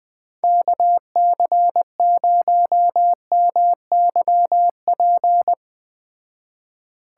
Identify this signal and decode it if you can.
Morse code, heard as a keyed tone, about 20 wpm, 700 Hz. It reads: KC0MYP
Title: Speed Racing - USA 2x3 Call Signs 17wpm